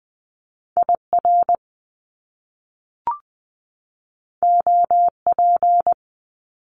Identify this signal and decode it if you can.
Morse code, heard as a keyed tone, about 20 wpm, 700 Hz. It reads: IR OP
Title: Top Two Letter Combinations in Words 20wpm